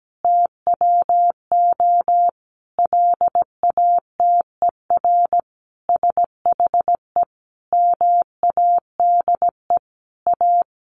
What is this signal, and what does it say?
Morse code, heard as a keyed tone, about 17 wpm, 700 Hz. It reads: TWO LATER SHE MADE A
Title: Sentences from Top 1100 Words 17wpm